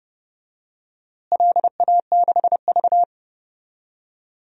Morse code, heard as a keyed tone, about 31 wpm, 700 Hz. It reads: LA6V
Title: DXCC-100 Call Signs - 4 Characters - Repeated 2x - 30wpm